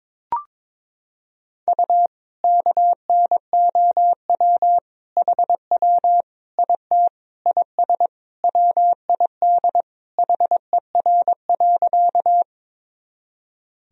Morse code, heard as a keyed tone, about 22 wpm, 700 Hz. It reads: U KNOW HW IT IS WID HER.
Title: Sentences from Top 100 Words and Abbreviations 22wpm